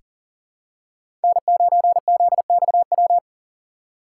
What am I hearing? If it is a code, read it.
Morse code, heard as a keyed tone, about 37 wpm, 700 Hz. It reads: N9ZXW
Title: USA Call Signs - 5 Characters - 2x Character Spacing - 40wpm